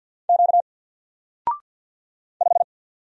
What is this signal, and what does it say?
Morse code, heard as a keyed tone, about 48 wpm, 700 Hz. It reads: Q 5
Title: Single Character (Letter:Number:Punctuation:Prosign) 50wpm